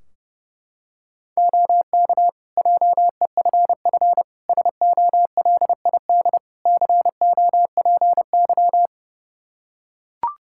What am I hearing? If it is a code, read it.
Morse code, heard as a keyed tone, about 30 wpm, 700 Hz. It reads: OK JEFF SOLID COPY
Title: K1USN SST QSO Practice 30wpm